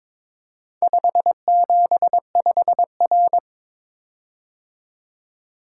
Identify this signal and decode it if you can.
Morse code, heard as a keyed tone, about 22 wpm, 700 Hz. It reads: 575R
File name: practice / 22wpm / Four Letter-Number Combinations - 22wpm